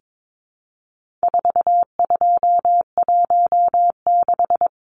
Morse code, heard as a keyed tone, about 22 wpm, 700 Hz. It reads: 4216
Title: Numbers - 4-Digit 22wpm